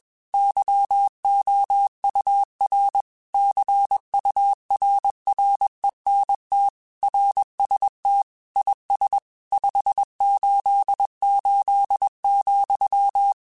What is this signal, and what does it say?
Morse code, heard as a keyed tone, about 21 wpm, 800 Hz. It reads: YOUR CURRENT RST IS 588,
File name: QSOs / QSOs-mp3s / WA7QOX 1@20